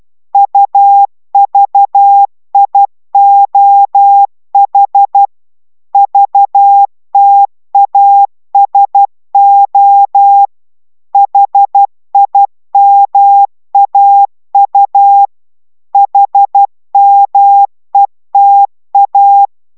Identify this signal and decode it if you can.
Morse code, heard as a keyed tone, about 12 wpm, 800 Hz. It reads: UVIOH VTASO HIMAU HMETA
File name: taviro tanulas / cw 69 05 10